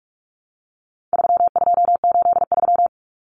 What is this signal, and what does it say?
Morse code, heard as a keyed tone, about 44 wpm, 700 Hz. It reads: 3283
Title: Common Cut Numbers - 4-Digit 45wpm